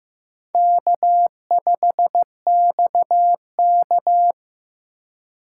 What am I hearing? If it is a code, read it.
Morse code, heard as a keyed tone, about 15 wpm, 700 Hz. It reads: K5XK
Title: K1USN SST QSO Practice - 15wpm